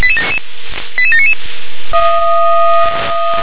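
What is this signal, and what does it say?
Five-tone selective call (ZVEI), heard as quick runs of five tones, tones of about 80 ms. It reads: 88799
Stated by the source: Five-tone ZVEI selective call received on an SDR.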